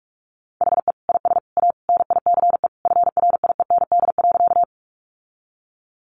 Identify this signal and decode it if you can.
Morse code, heard as a keyed tone, about 44 wpm, 700 Hz. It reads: HE IS A NICE FRIEND.
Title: Sentences from Top 1100 Words 45wpm